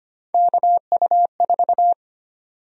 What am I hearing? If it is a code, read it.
Morse code, heard as a keyed tone, about 25 wpm, 700 Hz. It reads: KU4
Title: USA Call Sign Prefixes - 25wpm